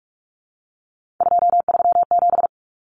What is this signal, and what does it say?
Morse code, heard as a keyed tone, about 44 wpm, 700 Hz. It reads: 237
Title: Numbers - 3-Digit 45wpm